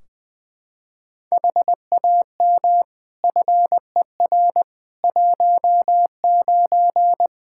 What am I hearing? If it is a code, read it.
Morse code, heard as a keyed tone, about 20 wpm, 700 Hz. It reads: HAM FER 19
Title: Practice QSO - Exchange 3 - More Info - 20wpm